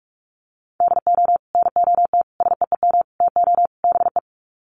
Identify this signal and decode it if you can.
Morse code, heard as a keyed tone, about 44 wpm, 700 Hz. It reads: DO NOT SEEM TO BE
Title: Sets of 5 Words Encore2 45wpm